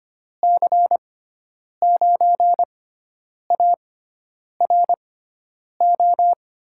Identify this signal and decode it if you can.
Morse code, heard as a keyed tone, about 25 wpm, spaced as for 9 wpm, 700 Hz. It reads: C9ARO
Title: USA Call Signs - 6 Characters - 6x Character Spacing - 25wpm